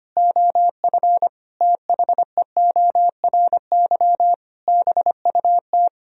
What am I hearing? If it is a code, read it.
Morse code, heard as a keyed tone, about 25 wpm, 700 Hz. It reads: OF THEORY BUT
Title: Sentences from Top 1000 Words - Review Part 4 - 25wpm